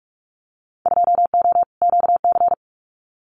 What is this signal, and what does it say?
Morse code, heard as a keyed tone, about 44 wpm, 700 Hz. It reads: 2O QC
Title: Winter Field Day Exchanges 45wpm